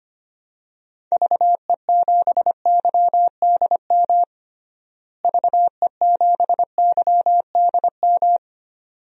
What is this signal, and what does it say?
Morse code, heard as a keyed tone, about 25 wpm, 700 Hz. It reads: VE7YDM VE7YDM
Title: DXCC-100 Call Signs - 6 Characters - Repeated 2x - 25wpm